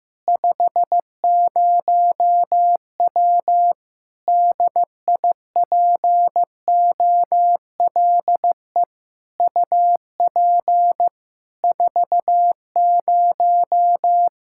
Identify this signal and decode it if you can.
Morse code, heard as a keyed tone, about 16 wpm, 700 Hz. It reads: 50W DIPOLE UP 40
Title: Practice QSO - Exchange 2 - Info - 15wpm